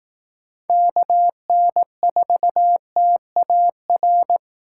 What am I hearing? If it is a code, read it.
Morse code, heard as a keyed tone, about 19 wpm, 700 Hz. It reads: KN4TAR
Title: Speed Racing - USA 2x3 Call Signs 15wpm